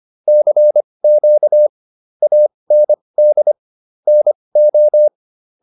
Morse code, heard as a keyed tone, about 25 wpm, spaced as for 20 wpm, 600 Hz. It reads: CQ AND NO